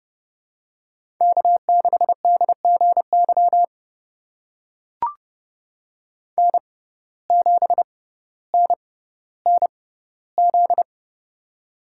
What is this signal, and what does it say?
Morse code, heard as a keyed tone, about 30 wpm, spaced as for 15 wpm, 700 Hz. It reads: K6DGY N7NNZ
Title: USA Call Signs - 5 Characters - 6x Character Spacing - 30wpm